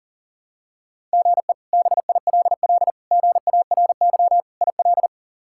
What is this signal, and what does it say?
Morse code, heard as a keyed tone, about 40 wpm, 700 Hz. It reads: GE BILL GARY IL
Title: K1USN SST QSO Practice - 40wpm